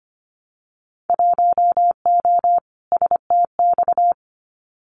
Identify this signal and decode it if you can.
Morse code, heard as a keyed tone, about 25 wpm, 700 Hz. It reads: 1O STX
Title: Winter Field Day Exchanges 25wpm